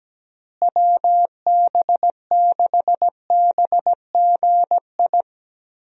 Morse code, heard as a keyed tone, about 17 wpm, 700 Hz. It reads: WB6BGI
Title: USA Call Signs - 6 Characters - Repeated 2x - 17wpm